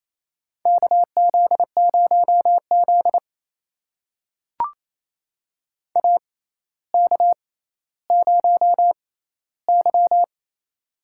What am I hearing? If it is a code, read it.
Morse code, heard as a keyed tone, about 28 wpm, spaced as for 14 wpm, 700 Hz. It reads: KZ0Z AK0Y
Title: USA Call Signs - 4 Characters - 6x Character Spacing - 28wpm